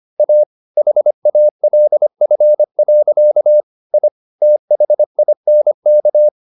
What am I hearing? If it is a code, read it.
Morse code, heard as a keyed tone, about 25 wpm, 600 Hz. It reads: A HALF. I THINK
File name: the old man and the sea part 1 25wpm